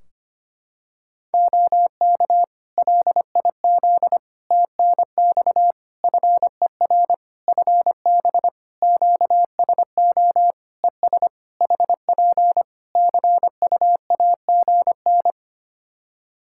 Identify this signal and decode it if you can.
Morse code, heard as a keyed tone, about 25 wpm, 700 Hz. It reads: OK LIZ TNX FER FB QSO ES HP CU AGN
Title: Practice QSO - Exchange 4 - Ending - 25wpm